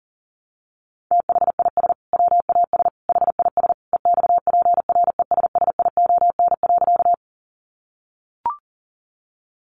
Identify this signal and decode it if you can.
Morse code, heard as a keyed tone, about 37 wpm, 700 Hz. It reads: THIS WAS HIS EXPRESSION.
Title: Sentences from Top 1200 Words 40wpm